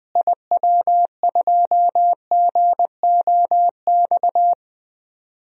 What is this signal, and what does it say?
Morse code, heard as a keyed tone, about 20 wpm, 700 Hz. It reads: IW2GOX
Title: DXCC-100 Call Signs - 6 Characters - Repeated 2x - 20wpm